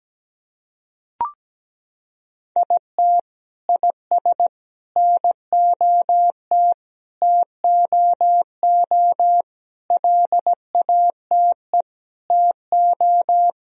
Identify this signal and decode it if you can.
Morse code, heard as a keyed tone, about 17 wpm, 700 Hz. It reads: IT IS NOT TOO LATE TO
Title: Sentences from Top 1000 Words - Review Part 2 - 17wpm